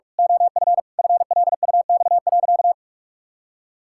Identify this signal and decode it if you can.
Morse code, heard as a keyed tone, about 46 wpm, 700 Hz. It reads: OF FLUX.
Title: Pangram Challenge v2 - 45wpm